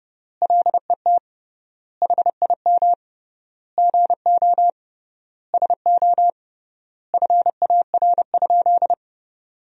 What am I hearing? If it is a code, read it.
Morse code, heard as a keyed tone, about 31 wpm, 700 Hz. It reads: LET HIM GO SO FAR?
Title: 3x Word-Spacing - Sentences from Top 300 Words - 30wpm